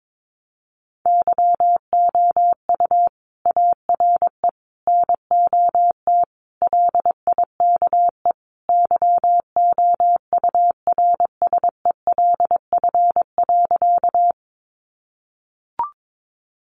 Morse code, heard as a keyed tone, about 22 wpm, 700 Hz. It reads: YOU ARE NOT LIKE YOURSELF.
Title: Sentences from Top 600 Words 22wpm